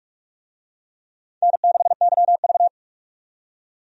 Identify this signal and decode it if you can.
Morse code, heard as a keyed tone, about 43 wpm, 700 Hz. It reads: N6YV
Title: USA Call Signs - 4 Characters - 45wpm